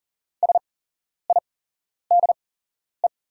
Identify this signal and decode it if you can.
Morse code, heard as a keyed tone, about 40 wpm, spaced as for 11 wpm, 700 Hz. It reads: SIDE
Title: 8x Character Spacing - Top 100 Words - 40wpm